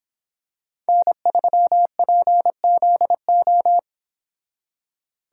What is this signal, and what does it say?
Morse code, heard as a keyed tone, about 25 wpm, 700 Hz. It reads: N3PZO